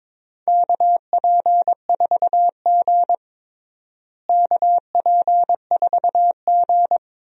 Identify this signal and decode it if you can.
Morse code, heard as a keyed tone, about 22 wpm, 700 Hz. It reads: KP4G KP4G
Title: DXCC-100 Call Signs - 4 Characters - Repeated 2x - 22wpm